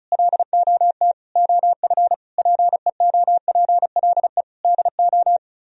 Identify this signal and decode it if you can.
Morse code, heard as a keyed tone, about 43 wpm, 700 Hz. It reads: LOT OF PEOPLE DO
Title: Speed-Racing - Sets of 5 Words 35wpm